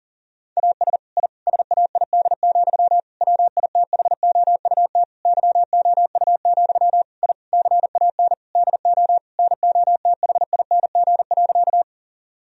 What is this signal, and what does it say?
Morse code, heard as a keyed tone, about 40 wpm, 700 Hz. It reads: AS I SAID, WITHOUT YOU, I CAN DO NOTHING.
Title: Sentences from Top 1000 Words - Review Part 3 - 40wpm